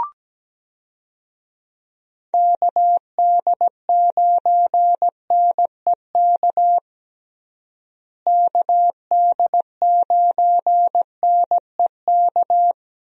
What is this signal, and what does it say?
Morse code, heard as a keyed tone, about 17 wpm, 700 Hz. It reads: KD9NEK KD9NEK
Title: USA Call Signs - 6 Characters - Repeated 3x - 17wpm